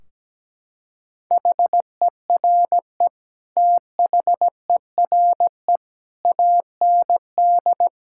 Morse code, heard as a keyed tone, about 17 wpm, 700 Hz. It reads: HERE THERE AND
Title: Sentences from Top 1600 Words 17wpm